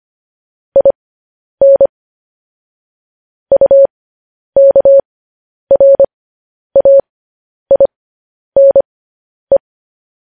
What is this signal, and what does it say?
Morse code, heard as a keyed tone, about 25 wpm, spaced as for 10 wpm, 550 Hz. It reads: IN UKRAINE
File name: podcast / News headlines-1x-25x10